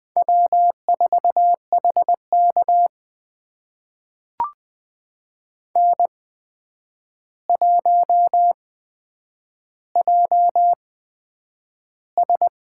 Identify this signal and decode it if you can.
Morse code, heard as a keyed tone, about 20 wpm, spaced as for 9 wpm, 700 Hz. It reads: W4HK N1JS
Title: USA Call Signs - 8x Character Spacing - 20wpm